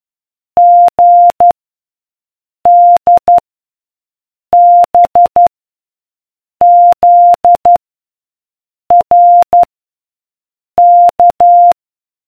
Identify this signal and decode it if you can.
Morse code, heard as a keyed tone, about 12 wpm, 700 Hz. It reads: G D B Z R K